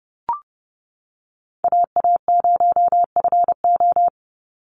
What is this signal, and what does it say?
Morse code, heard as a keyed tone, about 30 wpm, 700 Hz. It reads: AA0FO
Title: Call Signs Repeated 3x - 30wpm